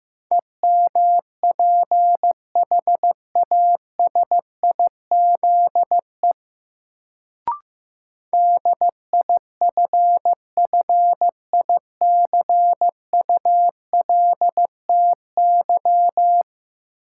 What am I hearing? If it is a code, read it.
Morse code, heard as a keyed tone, about 15 wpm, 700 Hz. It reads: EMPHASIZE DIFFICULTY
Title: Sentences from Top 1500 Words - New Word Drill - 15wpm